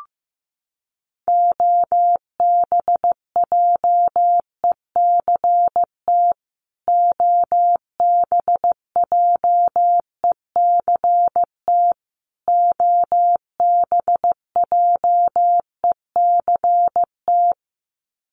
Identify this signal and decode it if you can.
Morse code, heard as a keyed tone, about 15 wpm, 700 Hz. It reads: OBJECT OBJECT OBJECT
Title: Top 500 Words Repeated 15wpm